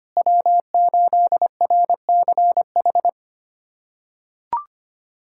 Morse code, heard as a keyed tone, about 25 wpm, 700 Hz. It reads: W8RCH
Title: USA Callsign DE Callsign - 5 Characters - 25wpm